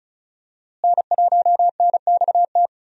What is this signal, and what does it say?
Morse code, heard as a keyed tone, about 36 wpm, 700 Hz. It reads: N1NXT
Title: USA Call Signs - 5 Characters - Repeated 2x - 35wpm